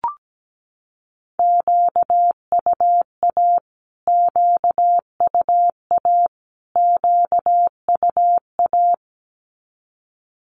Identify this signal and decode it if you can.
Morse code, heard as a keyed tone, about 17 wpm, 700 Hz. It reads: QUA QUA QUA
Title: Q-Codes Repeated 3x 17wpm